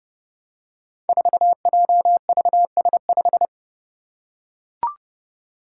Morse code, heard as a keyed tone, about 30 wpm, 700 Hz. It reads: 4JVS5
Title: Five Challenging Letter-Number Combinations 30wpm